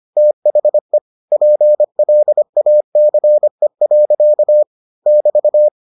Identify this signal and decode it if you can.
Morse code, heard as a keyed tone, about 25 wpm, 600 Hz. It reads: THE PLACE. <BT>